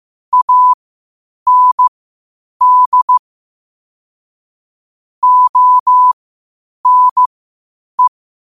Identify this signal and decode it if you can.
Morse code, heard as a keyed tone, about 15 wpm, spaced as for 8 wpm, 1000 Hz. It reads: AND ONE
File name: qotdfiles / 8wpm